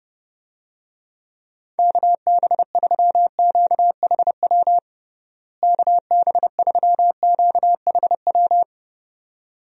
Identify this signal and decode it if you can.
Morse code, heard as a keyed tone, about 30 wpm, 700 Hz. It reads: KB3QHW KB3QHW
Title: USA Call Signs - 6 Characters - Repeated 2x - 30wpm